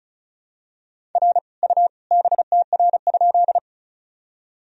Morse code, heard as a keyed tone, about 36 wpm, 700 Hz. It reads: R U BTR?
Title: Sentences from Top 400 Words and Abbreviations 35wpm